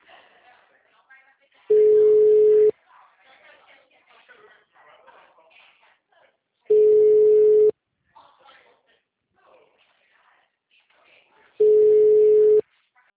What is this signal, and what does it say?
Ringback tone (Europe, 425 Hz), with speech.
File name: out- -2025- - - .24764